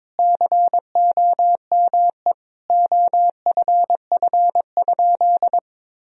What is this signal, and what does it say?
Morse code, heard as a keyed tone, about 22 wpm, 700 Hz. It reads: COME OFF?
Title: Sentences from Top 1000 Words - Review Part 3 - 22wpm